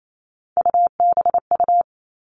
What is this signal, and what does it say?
Morse code, heard as a keyed tone, about 28 wpm, 700 Hz. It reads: UBU
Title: All Cut Numbers - 3-Digit 28wpm